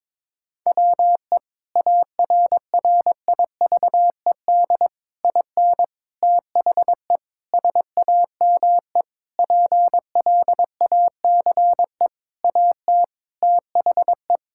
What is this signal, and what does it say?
Morse code, heard as a keyed tone, about 22 wpm, 700 Hz. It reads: WE ARRIVED IN THE SAME PLACE AT THE
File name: practice / 22wpm / Sentences with 8-Word Phrases - 22wpm